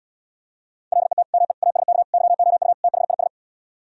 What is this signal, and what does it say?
Morse code, heard as a keyed tone, about 77 wpm, 700 Hz. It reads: LET ME ASK YOU THIS